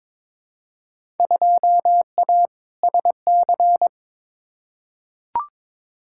Morse code, heard as a keyed tone, about 22 wpm, 700 Hz. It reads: 2A SC
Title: Field Day Exchanges - 22wpm